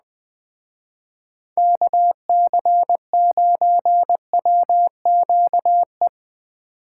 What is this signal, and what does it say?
Morse code, heard as a keyed tone, about 20 wpm, 700 Hz. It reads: KC9WQE
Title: Speed Racing - USA 2x3 Call Signs 17wpm